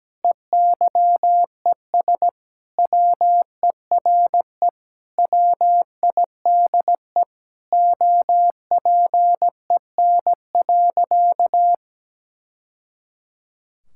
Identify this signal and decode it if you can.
Morse code, heard as a keyed tone, about 17 wpm, 700 Hz. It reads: EYES WERE WIDE OPEN.
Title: Sentences from Top 1000 Words - Review Part 6 - 17wpm